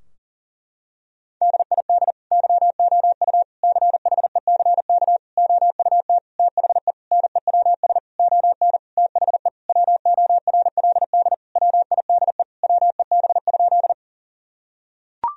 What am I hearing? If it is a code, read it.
Morse code, heard as a keyed tone, about 40 wpm, 700 Hz. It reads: DID YOU CHECK OUT THE NEWS ON THE WORLD WIDE WEB?
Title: Sentences from Top 1000 Words - Review Part 4 - 40wpm